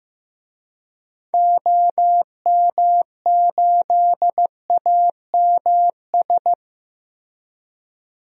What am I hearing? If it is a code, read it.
Morse code, heard as a keyed tone, about 15 wpm, 700 Hz. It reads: OM8AMS
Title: DXCC-100 Call Signs - 6 Characters - Repeated 3x - 15wpm